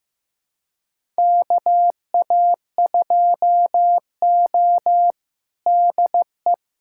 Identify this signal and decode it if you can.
Morse code, heard as a keyed tone, about 16 wpm, 700 Hz. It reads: KA2O DE
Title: USA Callsign DE Callsign - 4 Characters - 15wpm